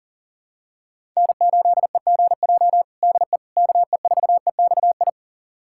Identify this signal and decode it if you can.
Morse code, heard as a keyed tone, about 40 wpm, 700 Hz. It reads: N8EGJ DE KE4EXI
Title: USA Callsign DE Callsign 40wpm